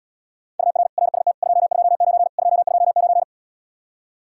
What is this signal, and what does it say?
Morse code, heard as a keyed tone, about 80 wpm, 700 Hz. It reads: UR RST 229 229